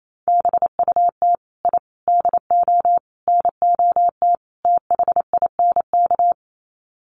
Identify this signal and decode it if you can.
Morse code, heard as a keyed tone, about 28 wpm, 700 Hz. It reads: BUT I DO NOT THINK
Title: Sets of 5 Words 28wpm